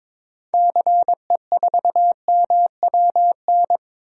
Morse code, heard as a keyed tone, about 22 wpm, 700 Hz. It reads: CE4MWN
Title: DXCC-100 Call Signs - 6 Characters - 22wpm